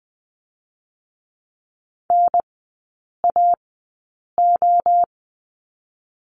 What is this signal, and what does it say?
Morse code, heard as a keyed tone, about 20 wpm, 700 Hz. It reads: N A O
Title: 023 Lesson 5e - Letter o - Three Characters 20wpm